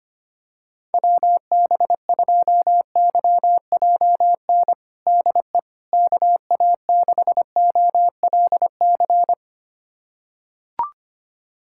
Morse code, heard as a keyed tone, about 25 wpm, 700 Hz. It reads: WB2YJN DE KA6OLC
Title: USA Callsign DE Callsign - 6 Characters - 25wpm